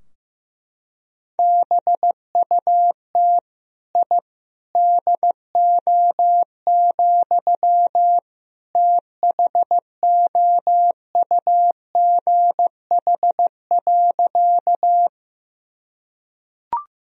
Morse code, heard as a keyed tone, about 15 wpm, 700 Hz. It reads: BUT I DO, THOUGH.
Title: Sentences from Top 1000 Words - Review Part 2 - 15wpm